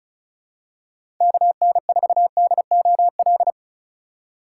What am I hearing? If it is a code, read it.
Morse code, heard as a keyed tone, about 34 wpm, 700 Hz. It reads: KN4DOL
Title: USA Call Signs - 6 Characters - 2x Character Spacing - 35wpm